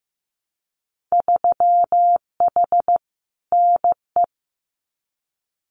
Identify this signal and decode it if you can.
Morse code, heard as a keyed tone, about 15 wpm, 700 Hz. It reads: 3H NE
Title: Winter Field Day Exchanges Encore 15wpm